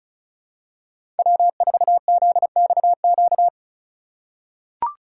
Morse code, heard as a keyed tone, about 36 wpm, 700 Hz. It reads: W4ZXQ
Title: USA Call Signs - 5 Characters - 8x Character Spacing - 35wpm